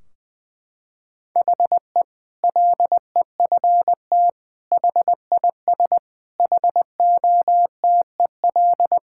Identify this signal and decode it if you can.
Morse code, heard as a keyed tone, about 21 wpm, 700 Hz. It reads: HE LEFT HIS HOTEL
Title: Sentences from Top 1600 Words 20wpm